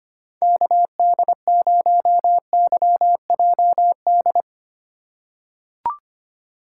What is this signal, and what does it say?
Morse code, heard as a keyed tone, about 25 wpm, 700 Hz. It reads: KD0YJD
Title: USA Call Signs - 2x Character Spacing - 25wpm